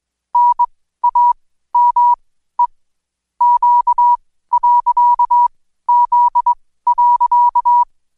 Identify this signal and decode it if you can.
Morse code, heard as a keyed tone, about 21 wpm, spaced as for 14 wpm, 970 Hz. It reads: NAME Q.Z.
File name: Code Quick / AudioDownld2 / Seg124-Day29-DayDrill 29